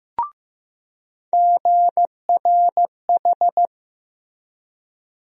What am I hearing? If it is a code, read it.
Morse code, heard as a keyed tone, about 15 wpm, 700 Hz. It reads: GRH
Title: USA Call Sign Postfixes - 3 Letters 15wpm